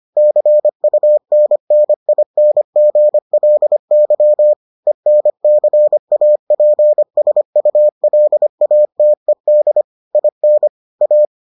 Morse code, heard as a keyed tone, about 25 wpm, 600 Hz. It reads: CUNNINGLY ENCAPSULATED IN A